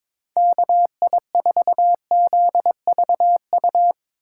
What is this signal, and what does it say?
Morse code, heard as a keyed tone, about 21 wpm, 700 Hz. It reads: KI4ZVU
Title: K1USN SST QSO Practice - 22wpm